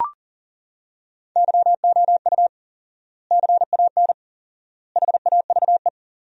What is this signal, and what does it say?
Morse code, heard as a keyed tone, about 40 wpm, 700 Hz. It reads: YOU CAN HAVE
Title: 4x Word-Spacing - Sentences from Top 100 Words - 40wpm